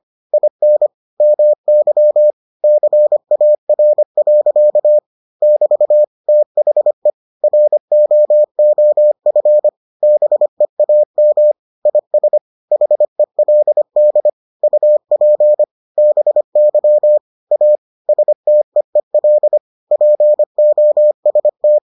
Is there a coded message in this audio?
Yes: Morse code from 0.3 to 21.8 s, heard as a keyed tone, about 25 wpm, 600 Hz. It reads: IN MY CAR. <BT> THE ROOF BEAM IS HELD UP BY A STEEL POST